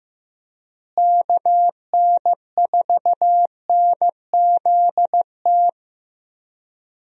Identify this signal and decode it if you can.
Morse code, heard as a keyed tone, about 15 wpm, 700 Hz. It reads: KN4NZT
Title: USA Call Signs - 6 Characters - Repeated 3x - 15wpm